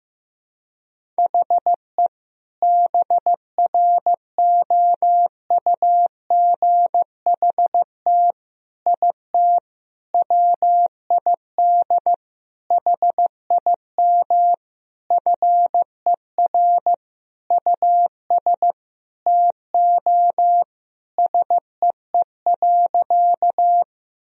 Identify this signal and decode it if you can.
Morse code, heard as a keyed tone, about 15 wpm, 700 Hz. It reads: HE BROUGHT IT WID HIM FER US TO SEE.
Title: Sentences from Top 500 Words and Abbreviations 15wpm